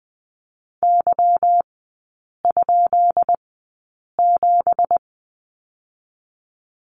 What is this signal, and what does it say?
Morse code, heard as a keyed tone, about 20 wpm, spaced as for 8 wpm, 700 Hz. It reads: Y?7
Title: 191 Lesson 31e - Letter v - Three Characters 20wpm